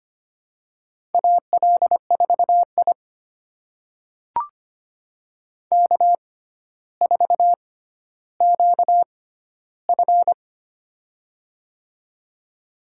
Morse code, heard as a keyed tone, about 25 wpm, spaced as for 13 wpm, 700 Hz. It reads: AL4I K4QF
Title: USA Call Signs - 4 Characters - 6x Character Spacing - 25wpm